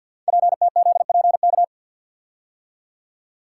Morse code, heard as a keyed tone, about 46 wpm, 700 Hz. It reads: PTZPK
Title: Code Groups - 5 Letters - 50wpm